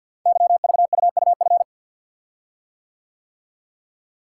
Morse code, heard as a keyed tone, about 52 wpm, 700 Hz. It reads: YVUUF
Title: Code Groups - 5 Letter:Numbers - 50wpm